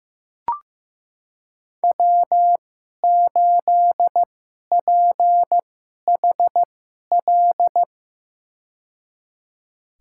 Morse code, heard as a keyed tone, about 15 wpm, spaced as for 11 wpm, 700 Hz. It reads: W8PHL
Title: USA Call Signs - 5 Characters - 2x Character Spacing - 15wpm